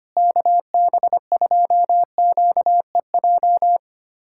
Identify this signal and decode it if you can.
Morse code, heard as a keyed tone, about 25 wpm, 700 Hz. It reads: KB2QEJ
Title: USA Callsign DE Callsign 25wpm